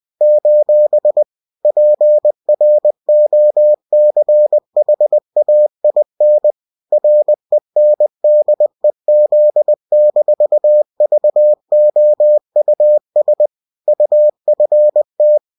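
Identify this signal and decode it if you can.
Morse code, heard as a keyed tone, about 20 wpm, 600 Hz. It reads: : PROCHAIN RENDEZ-VOUS UFT
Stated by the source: UFT bulletin sent in CW.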